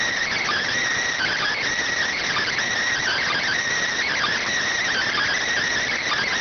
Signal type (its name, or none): Emergency Alert System (SAME) tones